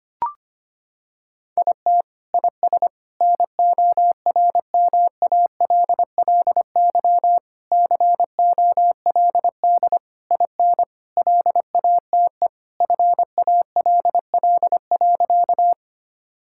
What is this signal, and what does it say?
Morse code, heard as a keyed tone, about 25 wpm, 700 Hz. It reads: IT IS NORMALLY COLD IN LATE FALL.
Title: Sentences from Top 1000 Words - Review Part 6 - 25wpm